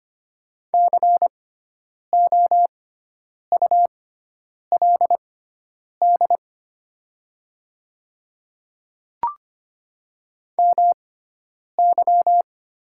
Morse code, heard as a keyed tone, about 25 wpm, spaced as for 9 wpm, 700 Hz. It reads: COULD MY
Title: 6x Character Spacing - Top 300 Words - 25wpm